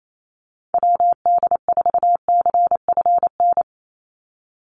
Morse code, heard as a keyed tone, about 28 wpm, 700 Hz. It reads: WD4CFN
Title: NA SOTA Chaser Callsigns 28wpm